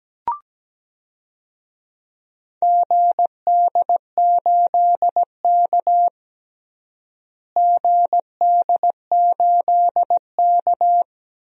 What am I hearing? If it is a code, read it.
Morse code, heard as a keyed tone, about 17 wpm, 700 Hz. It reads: GD8K GD8K
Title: DXCC-100 Call Signs - 4 Characters - Repeated 2x - 17wpm